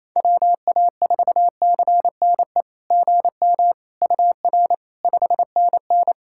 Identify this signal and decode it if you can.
Morse code, heard as a keyed tone, about 28 wpm, 700 Hz. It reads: WA4CNE GM UR 5NN
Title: SOTA QSO Practice 28wpm